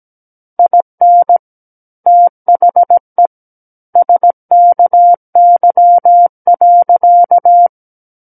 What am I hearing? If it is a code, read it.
Morse code, heard as a keyed tone, about 17 wpm, 700 Hz. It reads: IN THE SKY.
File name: Book One / 08-Chapter 8 - 17 WPM